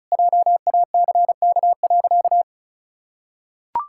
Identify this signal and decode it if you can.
Morse code, heard as a keyed tone, about 36 wpm, 700 Hz. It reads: JACK.
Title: Pangram Challenge v2 - 35wpm